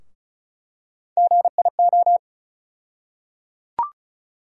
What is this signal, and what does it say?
Morse code, heard as a keyed tone, about 36 wpm, 700 Hz. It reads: GIO
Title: USA Call Sign Postfixes - 3 Letters 35wpm